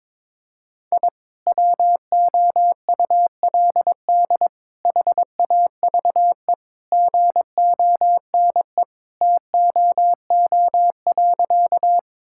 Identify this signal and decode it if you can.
Morse code, heard as a keyed tone, about 22 wpm, 700 Hz. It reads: I WOULD HAVE GONE TOO.
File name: practice / 22wpm / Sentences with 2-Word Phrases - 22wpm